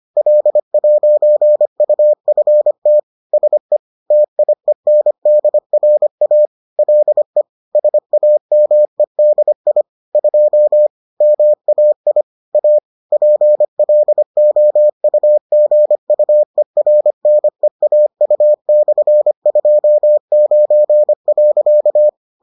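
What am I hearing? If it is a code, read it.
Morse code, heard as a keyed tone, about 25 wpm, 600 Hz. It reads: L'UFT SE TIENDRA LE SAMEDI 2 MAI A PLOUGUERNEAU/29.
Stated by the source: Bulletin sent in CW.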